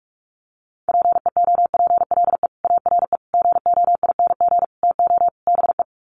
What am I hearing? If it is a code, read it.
Morse code, heard as a keyed tone, about 44 wpm, 700 Hz. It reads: PEOPLE ARE GOING TO BE
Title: Sets of 5 Words Encore 45wpm